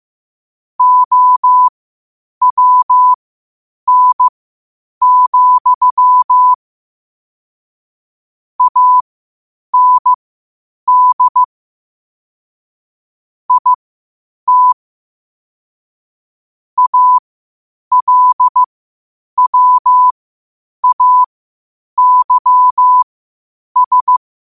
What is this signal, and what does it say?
Morse code, heard as a keyed tone, about 15 wpm, spaced as for 8 wpm, 1000 Hz. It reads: OWN, AND IT ALWAYS
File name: qotdfiles / 8wpm